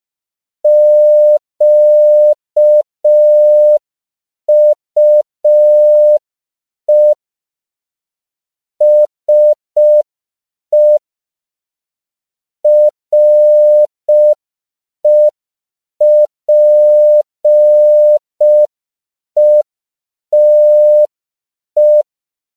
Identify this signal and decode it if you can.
Morse code, heard as a keyed tone, about 5 wpm, 600 Hz. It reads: QUE SE REPETE
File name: CW / 5PPM